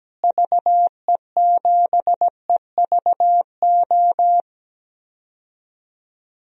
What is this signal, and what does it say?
Morse code, heard as a keyed tone, about 17 wpm, 700 Hz. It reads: VE7EVO
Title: DXCC-100 Call Signs - 6 Characters - Repeated 2x - 17wpm